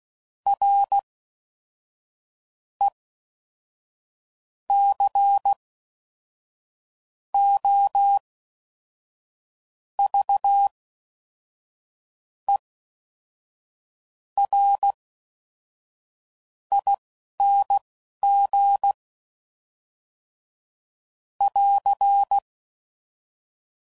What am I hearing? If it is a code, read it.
Morse code, heard as a keyed tone, about 16 wpm, spaced as for 5 wpm, 800 Hz. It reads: RECOVERING <AR>